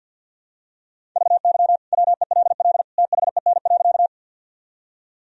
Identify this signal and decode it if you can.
Morse code, heard as a keyed tone, about 50 wpm, 700 Hz. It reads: VY WELL THEN.
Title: Sentences from Top 200 Words and Abbreviations 50wpm